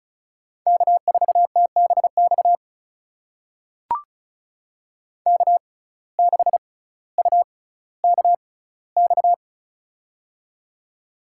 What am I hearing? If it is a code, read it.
Morse code, heard as a keyed tone, about 36 wpm, spaced as for 18 wpm, 700 Hz. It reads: K4TBX K6UKX
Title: USA Call Signs - 5 Characters - 6x Character Spacing - 35wpm